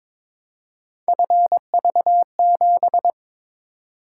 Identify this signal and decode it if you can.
Morse code, heard as a keyed tone, about 22 wpm, 700 Hz. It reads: FV7
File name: practice / 22wpm / Three Letter-Number Combinations - 22wpm